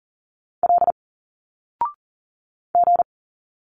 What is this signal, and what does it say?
Morse code, heard as a keyed tone, about 40 wpm, 700 Hz. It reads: L G
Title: Single Letters 40wpm